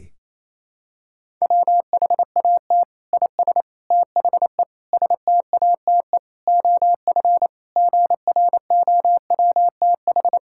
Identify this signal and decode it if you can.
Morse code, heard as a keyed tone, about 28 wpm, 700 Hz. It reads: WHAT IS THE STATE OF GROWTH
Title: Sentences from Top 1000 Words - Review Part 5 - 28wpm